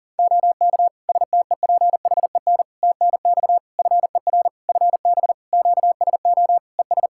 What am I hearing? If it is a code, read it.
Morse code, heard as a keyed tone, about 40 wpm, 700 Hz. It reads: OK STEPHEN TNX FER FB QSO ES
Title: K1USN SST QSO Practice 40wpm